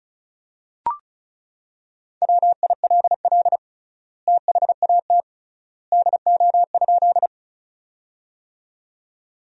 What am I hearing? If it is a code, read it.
Morse code, heard as a keyed tone, about 36 wpm, 700 Hz. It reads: WILL THAT DO?
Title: 3x Word-Spacing - Sentences from Top 100 Words - 35wpm